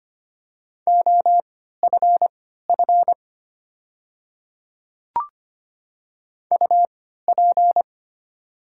Morse code, heard as a keyed tone, about 25 wpm, spaced as for 14 wpm, 700 Hz. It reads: OFF UP
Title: 3x Character Spacing - Top 200 Words - 25wpm